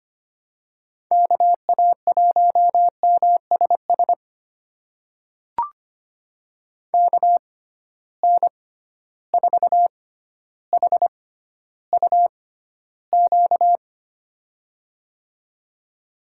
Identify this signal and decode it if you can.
Morse code, heard as a keyed tone, about 25 wpm, spaced as for 13 wpm, 700 Hz. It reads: KA1MSS KN4HUQ
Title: USA Call Signs - 6 Characters - 6x Character Spacing - 25wpm